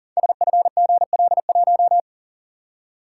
Morse code, heard as a keyed tone, about 38 wpm, 690 Hz. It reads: SFGL1
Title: Code Groups - 5 Letter:Numbers - 40wpm